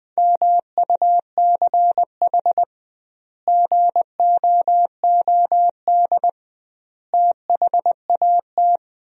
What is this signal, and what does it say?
Morse code, heard as a keyed tone, about 21 wpm, 700 Hz. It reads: MUCH GOOD THAT
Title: 2x Word-Spacing - Sentences from Top 200 Words - 20wpm